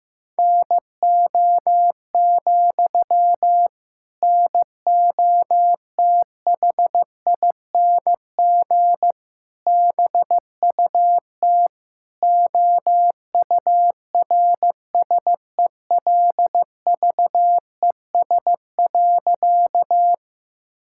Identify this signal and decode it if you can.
Morse code, heard as a keyed tone, about 15 wpm, 700 Hz. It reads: NO, NOTHING BUT OURSELVES.
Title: Sentences from Top 1600 Words 15wpm